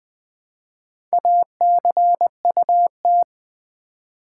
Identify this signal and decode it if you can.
Morse code, heard as a keyed tone, about 20 wpm, 700 Hz. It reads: ACUT
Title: Build A Word - 1 Letter At A Time - 5-Letter Words - 20wpm